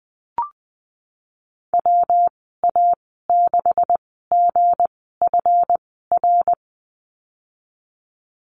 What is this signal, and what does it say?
Morse code, heard as a keyed tone, about 19 wpm, 700 Hz. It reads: W A 6 G F R
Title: 207 Lesson 33g - Letter j - Call Signs 20wpm